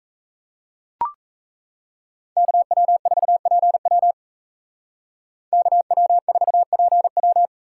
Morse code, heard as a keyed tone, about 36 wpm, 700 Hz. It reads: KW4PW KW4PW